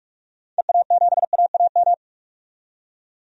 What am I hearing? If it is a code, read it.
Morse code, heard as a keyed tone, about 44 wpm, 690 Hz. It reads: EA7AAM
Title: DXCC-100 Call Signs - 6 Characters - 45wpm